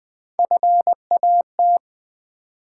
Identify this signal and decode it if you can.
Morse code, heard as a keyed tone, about 20 wpm, 700 Hz. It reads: FAT
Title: Three Letter Words - 20wpm